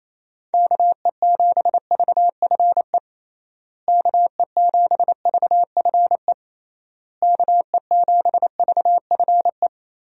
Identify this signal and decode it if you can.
Morse code, heard as a keyed tone, about 28 wpm, 700 Hz. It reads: KE7VFE KE7VFE KE7VFE
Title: USA Call Signs - 6 Characters - Repeated 3x - 28wpm